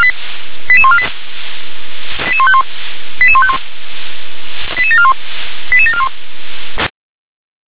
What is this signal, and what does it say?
Five-tone selective call (ZVEI), heard as quick runs of five tones, tones of about 70 ms. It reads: 88148 88151 88151 88741 88741